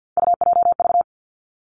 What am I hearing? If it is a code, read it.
Morse code, heard as a keyed tone, about 50 wpm, 700 Hz. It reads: UJV
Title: Three Challenging Letter-Number Combinations 50wpm